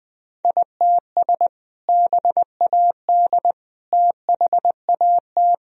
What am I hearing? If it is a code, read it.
Morse code, heard as a keyed tone, about 20 wpm, 700 Hz. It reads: ITS BAD THAT
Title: Sentences from Top 1000 Words - Review Part 1 - 20wpm